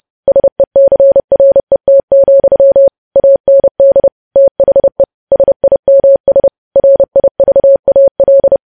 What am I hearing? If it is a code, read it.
Morse code, heard as a keyed tone, about 30 wpm, 550 Hz. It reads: SECRET, AND THE SIMS RIVAL
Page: news headlines in morse code at 30 wpm